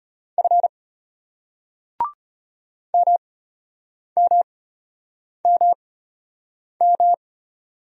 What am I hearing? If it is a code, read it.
Morse code, heard as a keyed tone, about 34 wpm, spaced as for 8 wpm, 700 Hz. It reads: F MMMM